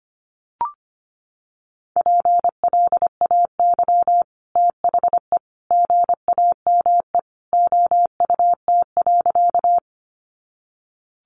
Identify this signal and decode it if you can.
Morse code, heard as a keyed tone, about 25 wpm, 700 Hz. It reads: PLAY THE GAME OUT.
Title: Sentences from Top 500 Words 25wpm